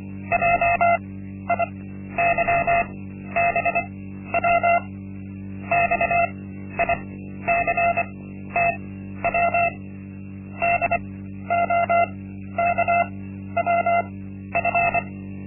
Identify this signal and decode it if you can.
Morse code, heard as a keyed tone, about 24 wpm, spaced as for 12 wpm, 690 Hz. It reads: JIYBW XICTW DOKWF
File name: Linksmakalnis-1